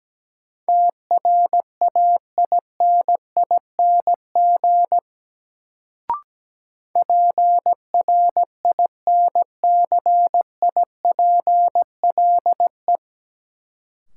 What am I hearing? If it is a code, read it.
Morse code, heard as a keyed tone, about 17 wpm, 700 Hz. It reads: TRAINING PRINCIPLE
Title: Sentences from Top 1400 Words - New Word Drill Alt - 17wpm